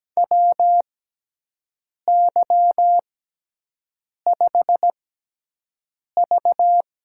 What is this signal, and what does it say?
Morse code, heard as a keyed tone, about 17 wpm, spaced as for 6 wpm, 700 Hz. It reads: WY5V
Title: USA Call Signs - 4 Characters - 6x Character Spacing - 17wpm